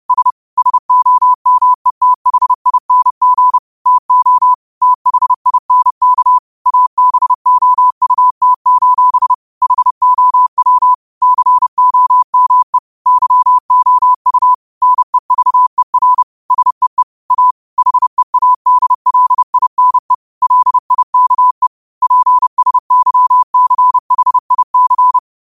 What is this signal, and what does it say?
Morse code, heard as a keyed tone, about 30 wpm, 1000 Hz. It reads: S SOMETHING TO THINK ABOUT: HOW COME YOU NEVER SEE A HEADLINE LIKE PSYCHIC
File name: qotdfiles / 30wpm